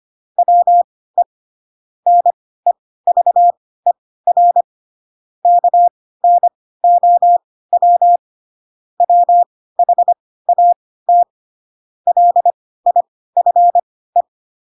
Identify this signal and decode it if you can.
Morse code, heard as a keyed tone, about 25 wpm, spaced as for 16 wpm, 700 Hz. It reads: WE NEVER KNOW WHAT LIFE
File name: Pinocchio / Pinocchio ch7 10 25 ik1hge